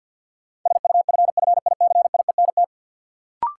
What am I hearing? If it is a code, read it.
Morse code, heard as a keyed tone, about 53 wpm, 700 Hz. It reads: SUFFICIENT
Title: Sentences from Top 1400 Words - New Word Drill Alt - 50wpm